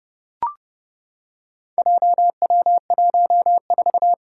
Morse code, heard as a keyed tone, about 30 wpm, 700 Hz. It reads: JW14
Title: Four Challenging Letter-Number Combinations 30wpm